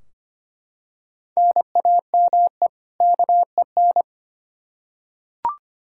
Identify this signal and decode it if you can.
Morse code, heard as a keyed tone, about 25 wpm, 700 Hz. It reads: NAME KEN
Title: K1USN SST QSO Practice 25wpm